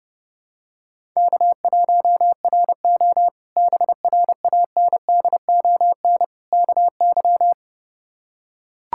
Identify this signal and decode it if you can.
Morse code, heard as a keyed tone, about 30 wpm, 700 Hz. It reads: K1RO BRANDON KY
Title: K1USN SST QSO Practice - 30wpm